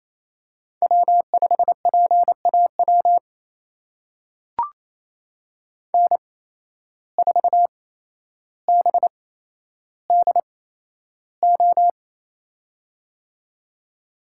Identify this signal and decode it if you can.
Morse code, heard as a keyed tone, about 28 wpm, spaced as for 12 wpm, 700 Hz. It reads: W5PAW N4BDO
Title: USA Call Signs - 5 Characters - 8x Character Spacing - 28wpm